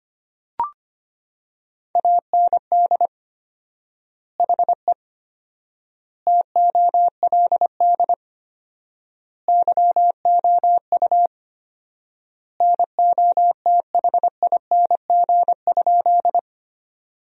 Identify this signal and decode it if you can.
Morse code, heard as a keyed tone, about 25 wpm, 700 Hz. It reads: AND HE TOLD YOU NOTHING?
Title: 4x Word-Spacing - Sentences from Top 500 Words - 25wpm